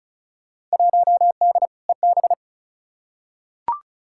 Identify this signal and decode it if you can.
Morse code, heard as a keyed tone, about 34 wpm, 700 Hz. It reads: 1D EB
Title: Field Day Exchanges - 35wpm